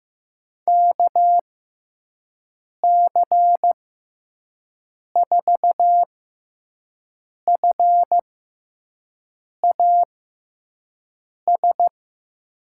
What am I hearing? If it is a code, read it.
Morse code, heard as a keyed tone, about 15 wpm, spaced as for 5 wpm, 700 Hz. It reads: KC4FAS
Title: USA Call Signs - 6 Characters - 6x Character Spacing - 15wpm